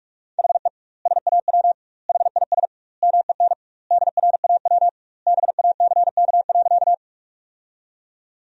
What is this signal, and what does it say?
Morse code, heard as a keyed tone, about 46 wpm, 700 Hz. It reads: HE SAW HIS MEN DRAW BACK.
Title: 2x Word-Spacing - Sentences from Top 300 Words - 45wpm